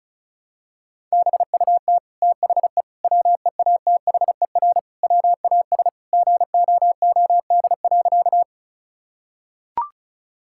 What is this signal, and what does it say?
Morse code, heard as a keyed tone, about 34 wpm, 700 Hz. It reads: BUT THE WEATHER WAS GOOD.
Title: Sentences from Top 1000 Words - Review Part 5 - 35wpm